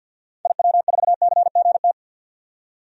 Morse code, heard as a keyed tone, about 52 wpm, 700 Hz. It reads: IW4CGT
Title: DXCC-100 Call Signs - 6 Characters - 50wpm